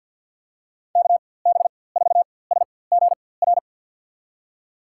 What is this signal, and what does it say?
Morse code, heard as a keyed tone, about 50 wpm, spaced as for 23 wpm, 700 Hz. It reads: KB4SGR
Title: USA Call Signs - 6 Characters - 4x Character Spacing - 50wpm